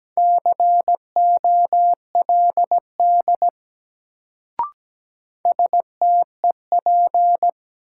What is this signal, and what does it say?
Morse code, heard as a keyed tone, about 17 wpm, 700 Hz. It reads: COLD STEP
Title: Sentences from Top 400 Words - New Word Drill - 17wpm